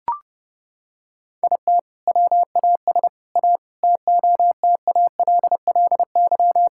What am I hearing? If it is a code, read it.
Morse code, heard as a keyed tone, about 32 wpm, 700 Hz. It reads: IT WAS A TOTALLY
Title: Sentences from Top 1600 Words 30wpm